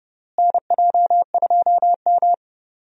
Morse code, heard as a keyed tone, about 30 wpm, 700 Hz. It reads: NJ2M
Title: USA Call Signs - 4 Characters - Repeated 2x - 30wpm